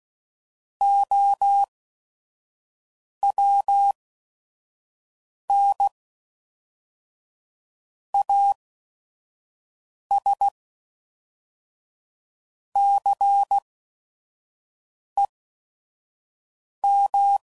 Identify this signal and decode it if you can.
Morse code, heard as a keyed tone, about 16 wpm, 800 Hz. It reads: O W N A S C E M